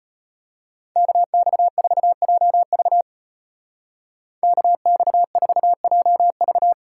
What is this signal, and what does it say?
Morse code, heard as a keyed tone, about 33 wpm, 700 Hz. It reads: KX4JV KX4JV